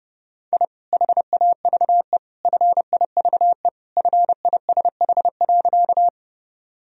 Morse code, heard as a keyed tone, about 30 wpm, 700 Hz. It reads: I HAVE FIVE FISH.
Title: Sentences from Top 1000 Words - Review Part 2 - 30wpm